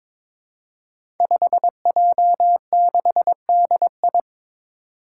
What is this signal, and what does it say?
Morse code, heard as a keyed tone, about 22 wpm, 700 Hz. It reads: 5J6DI
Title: Five Challenging Letter-Number Combinations 22wpm